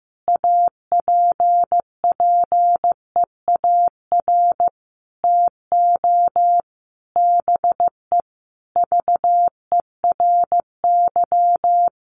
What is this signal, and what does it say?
Morse code, heard as a keyed tone, about 15 wpm, 700 Hz. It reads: APPEAR TO BE VERY
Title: Sentences from Top 1200 Words 15wpm